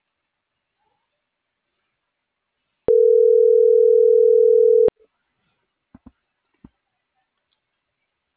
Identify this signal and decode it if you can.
Ringback tone (North America), 440 + 480 Hz.